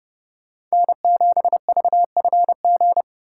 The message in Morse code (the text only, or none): N7VFG